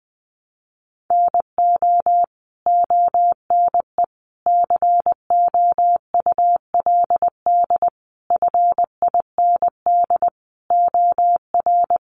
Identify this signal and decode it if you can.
Morse code, heard as a keyed tone, about 20 wpm, 700 Hz. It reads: NO ONE COULD FIND OR
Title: Sentences from Top 1100 Words 20wpm